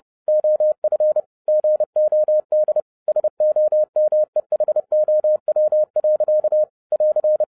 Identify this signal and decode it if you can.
Morse code, heard as a keyed tone, about 30 wpm, 610 Hz. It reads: OF GOD SOMEHOW. <AR>
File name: 2011-07-mp3 / n60 BACK FROM A BIG ACCIDENT 30wpm